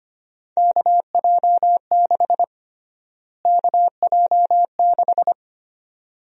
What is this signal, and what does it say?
Morse code, heard as a keyed tone, about 25 wpm, 700 Hz. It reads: KJ6 KJ6
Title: USA Call Sign Prefixes - Repeated 2x - 25wpm